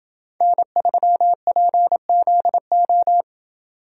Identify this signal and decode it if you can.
Morse code, heard as a keyed tone, about 27 wpm, 700 Hz. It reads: N3PZO